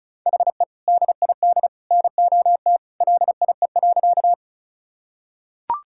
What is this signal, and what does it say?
Morse code, heard as a keyed tone, about 36 wpm, 700 Hz. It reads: HE DID NOT LIE.
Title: Sentences from Top 1600 Words 35wpm